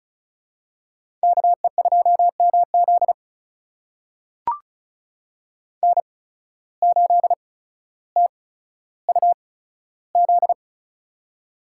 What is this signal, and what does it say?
Morse code, heard as a keyed tone, about 36 wpm, spaced as for 15 wpm, 700 Hz. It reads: KE2MZ N8TUZ
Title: USA Call Signs - 5 Characters - 8x Character Spacing - 35wpm